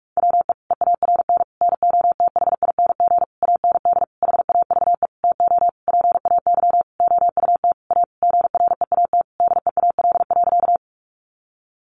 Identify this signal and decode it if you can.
Morse code, heard as a keyed tone, about 44 wpm, 700 Hz. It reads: WE EARN NOTHING AND HAVE TO PAY OUT A GREAT DEAL.
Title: Sentences from Top 1500 Words 45wpm